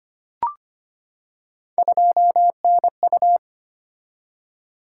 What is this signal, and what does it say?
Morse code, heard as a keyed tone, about 25 wpm, 700 Hz. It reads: 2NU
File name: Three Letter-Number Combinations - 25wpm